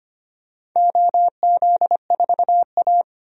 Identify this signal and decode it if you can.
Morse code, heard as a keyed tone, about 25 wpm, 700 Hz. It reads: OZ4A
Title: DXCC-100 Call Signs - 4 Characters - 25wpm